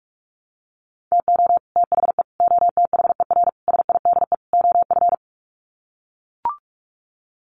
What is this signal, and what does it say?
Morse code, heard as a keyed tone, about 44 wpm, 700 Hz. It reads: TO THE OTHER SIDE OF
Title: Sets of 5 Words Encore 45wpm